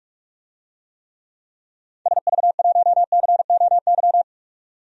Morse code, heard as a keyed tone, about 43 wpm, 700 Hz. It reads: SV1COY
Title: DXCC-100 Call Signs - 6 Characters - 45wpm